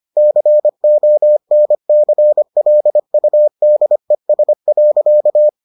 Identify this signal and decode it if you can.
Morse code, heard as a keyed tone, about 25 wpm, 600 Hz. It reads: CONCLUDES.